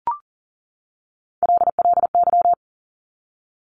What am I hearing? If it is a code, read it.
Morse code, heard as a keyed tone, about 39 wpm, 700 Hz. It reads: LLY
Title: Top Three Letter Combinations in Words 40wpm